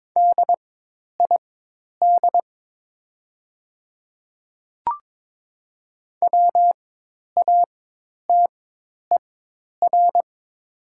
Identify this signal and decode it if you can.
Morse code, heard as a keyed tone, about 22 wpm, spaced as for 10 wpm, 700 Hz. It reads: DID WATER
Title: 4x Character Spacing - Top 100 Words - 22wpm